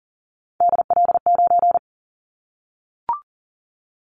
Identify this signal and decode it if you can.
Morse code, heard as a keyed tone, about 40 wpm, 700 Hz. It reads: DL9
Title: Callsign Prefixes for Top 25 DXCC - Repeated 3x - 40wpm